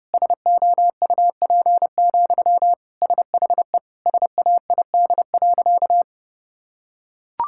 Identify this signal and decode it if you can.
Morse code, heard as a keyed tone, about 32 wpm, 700 Hz. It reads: SOUP, SHE SAID.
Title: Sentences from Top 1000 Words - Review Part 6 - 30wpm